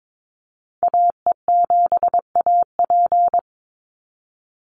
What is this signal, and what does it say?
Morse code, heard as a keyed tone, about 22 wpm, 700 Hz. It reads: AE7AP
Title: NA SOTA Activator Callsigns 22wpm